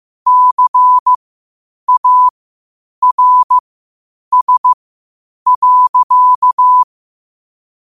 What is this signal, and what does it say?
Morse code, heard as a keyed tone, about 15 wpm, spaced as for 8 wpm, 1000 Hz. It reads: CARS.
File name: qotdfiles / 8wpm